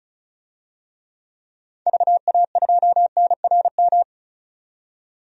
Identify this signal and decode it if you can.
Morse code, heard as a keyed tone, about 36 wpm, 700 Hz. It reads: VA2NRM
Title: DXCC-100 Call Signs - 6 Characters - 35wpm